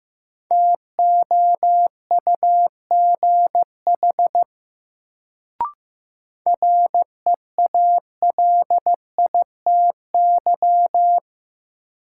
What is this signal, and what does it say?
Morse code, heard as a keyed tone, about 15 wpm, 700 Hz. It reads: TOUGH REALITY
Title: Sentences from Top 1200 Words - New Word Drill - 15wpm